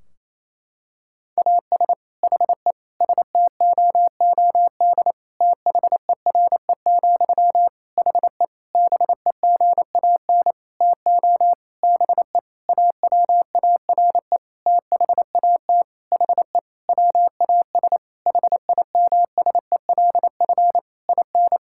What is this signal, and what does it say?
Morse code, heard as a keyed tone, about 28 wpm, 700 Hz. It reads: AS HE STOOD THERE, HE BEGAN TO BE AWARE THAT HE WAS HIMSELF IN
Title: Sentences from Top 1600 Words 28wpm